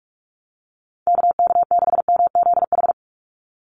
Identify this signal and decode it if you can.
Morse code, heard as a keyed tone, about 44 wpm, 700 Hz. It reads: KK6MZH
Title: USA Call Signs 45wpm